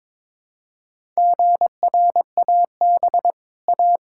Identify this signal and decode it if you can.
Morse code, heard as a keyed tone, about 22 wpm, 700 Hz. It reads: GRAB A
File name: practice / 22wpm / Easy Sentences with Hints - 22wpm